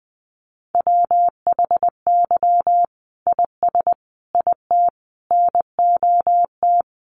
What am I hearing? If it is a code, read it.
Morse code, heard as a keyed tone, about 20 wpm, 700 Hz. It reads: WHY IS IT NOT
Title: Sentences from Top 1400 Words 20wpm